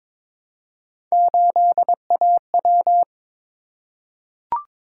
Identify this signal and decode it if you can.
Morse code, heard as a keyed tone, about 23 wpm, 700 Hz. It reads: 8AW
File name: practice / 22wpm / Three Letter-Number Combinations - 22wpm